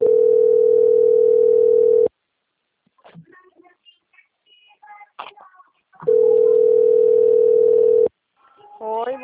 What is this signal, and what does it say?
Ringback tone (North America), 440 + 480 Hz, with speech.
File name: out- - - - - .414